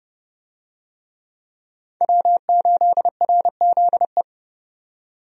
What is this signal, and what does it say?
Morse code, heard as a keyed tone, about 30 wpm, 700 Hz. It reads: W8RZE
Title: USA Call Signs - 5 Characters - 30wpm